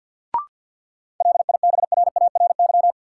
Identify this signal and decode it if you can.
Morse code, heard as a keyed tone, about 52 wpm, 700 Hz. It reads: LIBRARY
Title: Sentences from Top 600 Words - New Word Drill - 50wpm